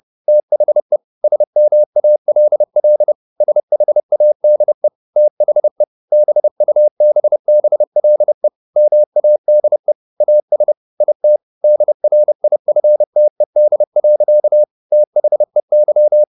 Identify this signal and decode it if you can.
Morse code, heard as a keyed tone, about 30 wpm, 600 Hz. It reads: THE SMALL SHADE THE BUBBLE MADE AS IT DRIFTED. THEY
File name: the old man and the sea / the old man and the sea 1 30wpm